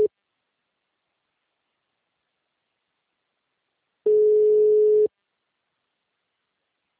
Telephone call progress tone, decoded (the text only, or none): ringback tone